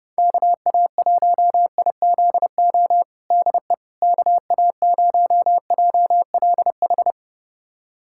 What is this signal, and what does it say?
Morse code, heard as a keyed tone, about 30 wpm, 700 Hz. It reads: KA1IZO DE KA0JLH
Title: USA Callsign DE Callsign - 6 Characters - 30wpm